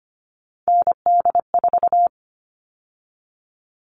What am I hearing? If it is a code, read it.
Morse code, heard as a keyed tone, about 25 wpm, 700 Hz. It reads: ND4
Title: All Cut Numbers - 3-Digit 25wpm